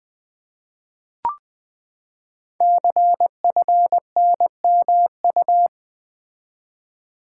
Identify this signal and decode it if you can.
Morse code, heard as a keyed tone, about 21 wpm, 700 Hz. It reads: CFNMU
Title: Code Groups - 5 Letters - 20wpm